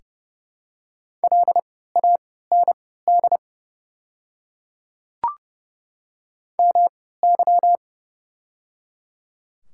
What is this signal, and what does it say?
Morse code, heard as a keyed tone, about 30 wpm, spaced as for 17 wpm, 700 Hz. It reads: LAND MY
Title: 3x Character Spacing - Top 300 Words - 30wpm